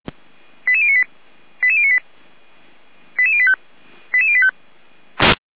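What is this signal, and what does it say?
Five-tone selective call (ZVEI), heard as quick runs of five tones, tones of about 70 ms. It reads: 88098 88098 88085 88085
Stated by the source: ZVEI five-tone fire brigade alert received with an SDR by feuerwehr-orlamuende.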